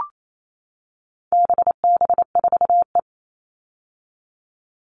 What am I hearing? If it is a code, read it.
Morse code, heard as a keyed tone, about 28 wpm, 700 Hz. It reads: BB4E
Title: All Cut Numbers - 4-Digit 28wpm